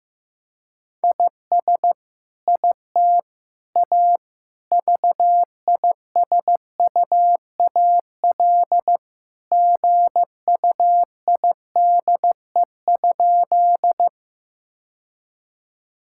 Morse code, heard as a keyed tone, about 16 wpm, 700 Hz. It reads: IS IT A VISUAL GUIDE?
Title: Sentences from Top 1000 Words - Review Part 6 - 15wpm